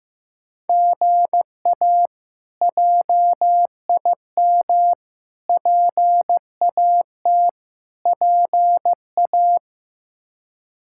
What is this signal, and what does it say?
Morse code, heard as a keyed tone, about 15 wpm, 700 Hz. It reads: GA JIM PAT PA
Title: K1USN SST QSO Practice - 15wpm